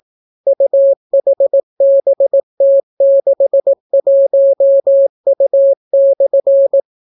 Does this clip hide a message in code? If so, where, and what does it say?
Yes: Morse code from 0.5 to 6.8 s, heard as a keyed tone, about 18 wpm, 550 Hz. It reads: UHBT61U/